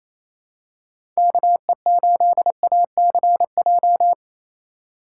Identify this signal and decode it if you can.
Morse code, heard as a keyed tone, about 28 wpm, 700 Hz. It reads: KE8ACJ
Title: USA Call Signs - 6 Characters - 4x Character Spacing - 28wpm